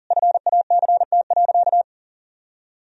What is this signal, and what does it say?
Morse code, heard as a keyed tone, about 40 wpm, 700 Hz. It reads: FACT.
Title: Sentences from Top 1000 Words - Review Part 3 - 40wpm